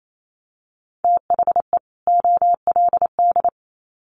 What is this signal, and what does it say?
Morse code, heard as a keyed tone, about 28 wpm, 700 Hz. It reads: THE OLD
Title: Sets of 2 Words 28wpm